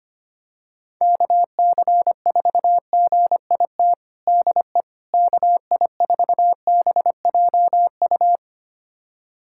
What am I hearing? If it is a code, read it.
Morse code, heard as a keyed tone, about 25 wpm, 700 Hz. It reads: KC4GIT DE KI4BJU
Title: USA Callsign DE Callsign 25wpm